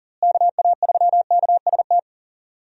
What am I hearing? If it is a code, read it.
Morse code, heard as a keyed tone, about 40 wpm, 700 Hz. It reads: KA3KST
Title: USA Call Signs - 6 Characters - Repeated 3x - 40wpm